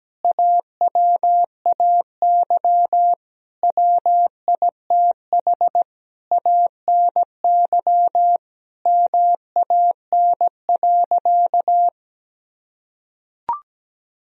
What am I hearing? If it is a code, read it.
Morse code, heard as a keyed tone, about 17 wpm, 700 Hz. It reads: AWAY WITH ANY MAN.
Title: Sentences from Top 1000 Words - Review Part 1 - 17wpm